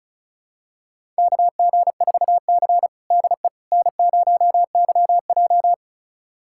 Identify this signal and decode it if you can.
Morse code, heard as a keyed tone, about 36 wpm, 700 Hz. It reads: KG4C DE N0YJ
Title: USA Callsign DE Callsign - 4 Characters - 35wpm